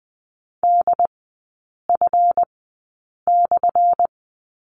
Morse code, heard as a keyed tone, about 20 wpm, spaced as for 8 wpm, 700 Hz. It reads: DF/
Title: 218 Lesson 35e - Number 0 - Three Characters 20wpm